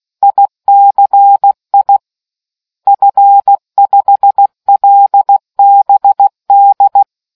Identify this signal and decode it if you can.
Morse code, heard as a keyed tone, about 16 wpm, 800 Hz. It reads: ICI F5LBD